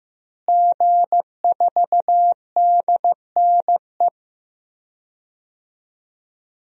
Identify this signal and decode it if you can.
Morse code, heard as a keyed tone, about 15 wpm, 700 Hz. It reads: G4DNE
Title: DXCC-100 Call Signs - 5 Characters - Repeated 3x - 15wpm